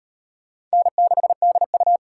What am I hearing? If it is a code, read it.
Morse code, heard as a keyed tone, about 35 wpm, 700 Hz. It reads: N6DU